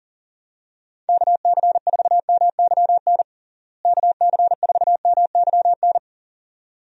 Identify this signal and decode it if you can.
Morse code, heard as a keyed tone, about 40 wpm, 700 Hz. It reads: KC4MYN KC4MYN
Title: USA Call Signs - 6 Characters - Repeated 2x - 40wpm